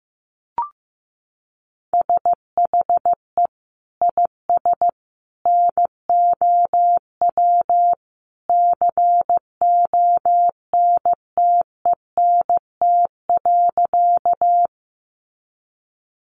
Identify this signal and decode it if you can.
Morse code, heard as a keyed tone, about 15 wpm, 700 Hz. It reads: SHE IS NOW CONTENT.
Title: Sentences from Top 900 Words 15wpm